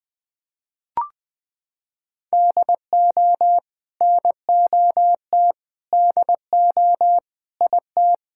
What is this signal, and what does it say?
Morse code, heard as a keyed tone, about 20 wpm, 700 Hz. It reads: DO NOT DO IT
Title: Sentences from Top 1000 Words - Review Part 6 - 20wpm